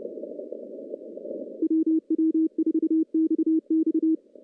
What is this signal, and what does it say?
Morse code, heard as a keyed tone, about 30 wpm, 320 Hz. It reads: WW4XX
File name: CW-SS-k5tr